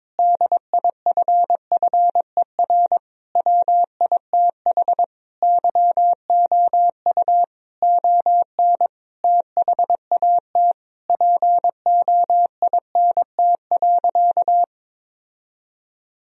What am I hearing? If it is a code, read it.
Morse code, heard as a keyed tone, about 22 wpm, 700 Hz. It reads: DIFFER WITH YOU ON THAT POINT.
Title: Sentences from Top 1000 Words - Review Part 1 - 22wpm